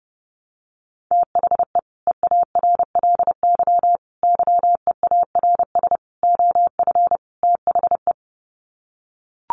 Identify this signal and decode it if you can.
Morse code, heard as a keyed tone, about 28 wpm, 700 Hz. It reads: THE EARLY YEARS OF THE
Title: Sets of 5 Words Encore 30wpm